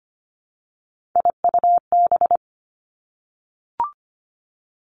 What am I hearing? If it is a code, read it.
Morse code, heard as a keyed tone, about 25 wpm, 700 Hz. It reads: IUB
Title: Three Challenging Letter-Number Combinations 25wpm